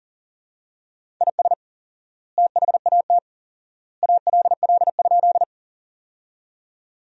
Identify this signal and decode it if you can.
Morse code, heard as a keyed tone, about 40 wpm, 700 Hz. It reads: IS THAT ALL?
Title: 4x Word-Spacing - Sentences from Top 100 Words - 40wpm